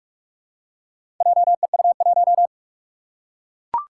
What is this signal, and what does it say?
Morse code, heard as a keyed tone, about 45 wpm, 700 Hz. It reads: JEU1
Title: Four Challenging Letter-Number Combinations 45wpm